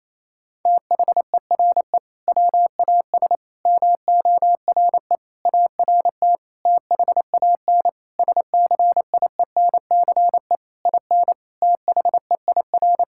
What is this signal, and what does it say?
Morse code, heard as a keyed tone, about 27 wpm, 700 Hz. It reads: THERE WAS MORE ART THAN SCIENCE IN THEIR
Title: Sentences from Top 1000 Words - Review Part 6 - 28wpm